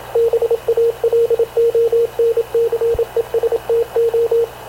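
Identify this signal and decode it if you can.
Morse code, heard as a keyed tone, about 27 wpm, 460 Hz, with radio noise. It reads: BALONCESTO
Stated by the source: Havana Radio CLA press (PX) broadcast, recorded by Radio Officer Paul Dumesnil aboard mv CNAV QUEST/CZDO.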